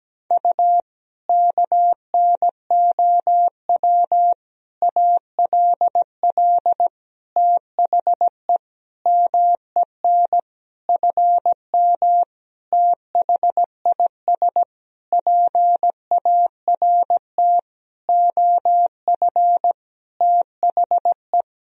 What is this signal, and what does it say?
Morse code, heard as a keyed tone, about 17 wpm, 700 Hz. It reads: U KNOW ALL THE MEN FM THIS PART OF THE
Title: Sentences from Top 300 Words and Abbreviations 17wpm